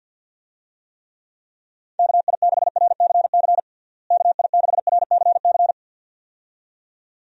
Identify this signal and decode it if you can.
Morse code, heard as a keyed tone, about 50 wpm, 700 Hz. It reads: KI6RCC KI6RCC
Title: USA Call Signs - 6 Characters - Repeated 2x - 50wpm